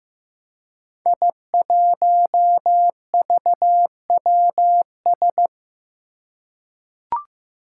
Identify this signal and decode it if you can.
Morse code, heard as a keyed tone, about 15 wpm, 700 Hz. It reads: I1VWS
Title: Five Challenging Letter-Number Combinations 15wpm